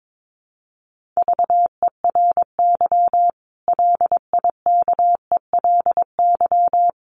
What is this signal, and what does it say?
Morse code, heard as a keyed tone, about 22 wpm, 700 Hz. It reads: VERY LIKELY
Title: Sentences from Top 600 Words 22wpm